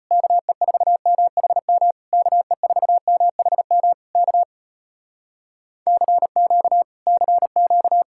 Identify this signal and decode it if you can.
Morse code, heard as a keyed tone, about 34 wpm, 700 Hz. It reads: KE4MHM KE4MHM K CQ CQ